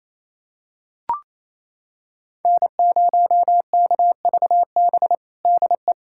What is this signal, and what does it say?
Morse code, heard as a keyed tone, about 28 wpm, 700 Hz. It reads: N0KVB DE
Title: USA Callsign DE Callsign - 5 Characters - 28wpm